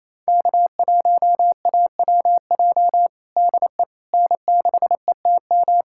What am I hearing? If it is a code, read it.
Morse code, heard as a keyed tone, about 28 wpm, 700 Hz. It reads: K1AWJ DE N6ETM
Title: USA Callsign DE Callsign 28wpm